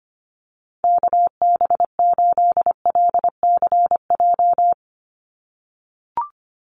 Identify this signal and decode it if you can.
Morse code, heard as a keyed tone, about 25 wpm, 700 Hz. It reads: KB8LCJ
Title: USA Call Signs 25wpm